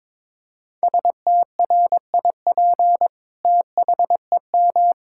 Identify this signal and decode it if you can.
Morse code, heard as a keyed tone, about 22 wpm, 700 Hz. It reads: STRIP THEM
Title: Sentences from Top 1600 Words 22wpm